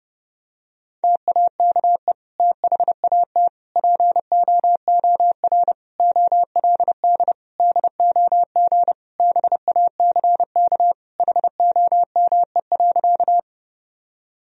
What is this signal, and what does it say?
Morse code, heard as a keyed tone, about 32 wpm, 700 Hz. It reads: TAKE THAT POOR OLD DOG BACK HOME.
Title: Sentences from Top 1000 Words - Review Part 2 - 30wpm